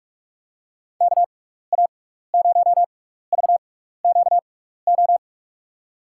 Morse code, heard as a keyed tone, about 46 wpm, spaced as for 16 wpm, 700 Hz. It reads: KA0VQO
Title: USA Call Signs - 6 Characters - 6x Character Spacing - 45wpm